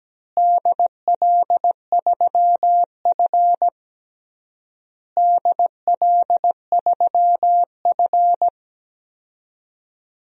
Morse code, heard as a keyed tone, about 17 wpm, 700 Hz. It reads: DL3F DL3F
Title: DXCC-100 Call Signs - 4 Characters - Repeated 3x - 17wpm